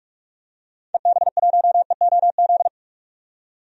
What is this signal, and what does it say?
Morse code, heard as a keyed tone, about 43 wpm, 700 Hz. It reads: EB1EOZ
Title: DXCC-100 Call Signs - 6 Characters - 45wpm